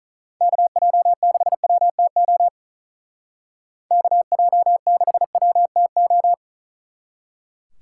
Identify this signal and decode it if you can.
Morse code, heard as a keyed tone, about 44 wpm, 700 Hz. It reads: KJ6WTO KJ6WTO